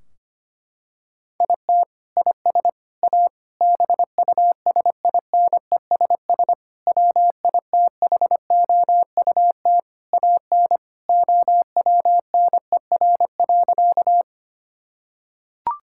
Morse code, heard as a keyed tone, about 25 wpm, 700 Hz. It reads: IT IS A BUSINESS WITHOUT AN OWNER.
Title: Sentences from Top 1600 Words 25wpm